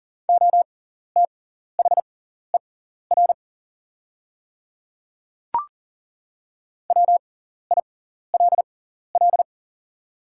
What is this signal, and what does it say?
Morse code, heard as a keyed tone, about 40 wpm, spaced as for 14 wpm, 700 Hz. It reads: OTHER WILL
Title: 6x Character Spacing - Top 300 Words - 40wpm